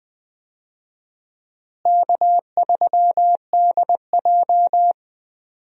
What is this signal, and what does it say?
Morse code, heard as a keyed tone, about 20 wpm, 700 Hz. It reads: K3DJ
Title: USA Call Signs - 4 Characters - Repeated 2x - 20wpm